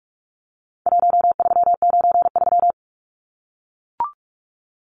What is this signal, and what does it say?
Morse code, heard as a keyed tone, about 44 wpm, 700 Hz. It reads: 1393
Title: Numbers - 4-Digit 45wpm